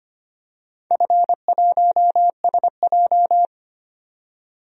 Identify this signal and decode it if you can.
Morse code, heard as a keyed tone, about 25 wpm, 700 Hz. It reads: F1SJ
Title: DXCC-100 Call Signs - 4 Characters - 25wpm